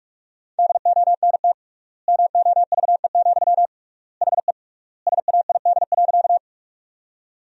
Morse code, heard as a keyed tone, about 46 wpm, 700 Hz. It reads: DONT MOVE, HE SAID.
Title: 3x Word-Spacing - Sentences from Top 300 Words - 45wpm